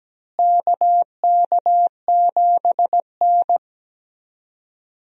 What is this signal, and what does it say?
Morse code, heard as a keyed tone, about 17 wpm, 700 Hz. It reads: KK7N
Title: USA Call Signs - 4 Characters - Repeated 3x - 17wpm